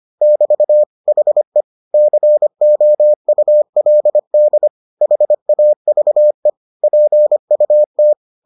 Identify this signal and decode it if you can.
Morse code, heard as a keyed tone, about 25 wpm, 600 Hz. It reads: <BT> HE COULD HAVE PUT